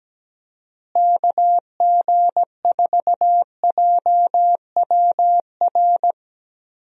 Morse code, heard as a keyed tone, about 17 wpm, 700 Hz. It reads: KG4JWR
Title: USA Call Signs - 6 Characters - Repeated 2x - 17wpm